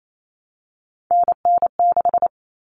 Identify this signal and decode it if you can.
Morse code, heard as a keyed tone, about 28 wpm, 700 Hz. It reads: NN6
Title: Common Cut Numbers - 3-Digit 28wpm